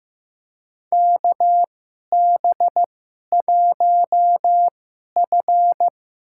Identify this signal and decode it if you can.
Morse code, heard as a keyed tone, about 15 wpm, spaced as for 11 wpm, 700 Hz. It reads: KB1F
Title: USA Call Signs - 4 Characters - 2x Character Spacing - 15wpm